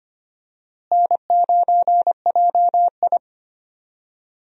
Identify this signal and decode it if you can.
Morse code, heard as a keyed tone, about 25 wpm, 700 Hz. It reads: N9JI
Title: USA Call Signs - 4 Characters - Repeated 3x - 25wpm